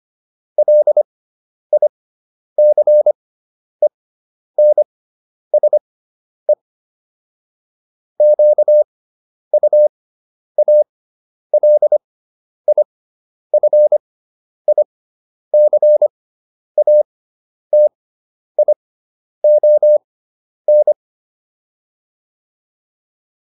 Morse code, heard as a keyed tone, about 25 wpm, spaced as for 10 wpm, 600 Hz. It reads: LICENSE QUALIFICATION
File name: SS213 10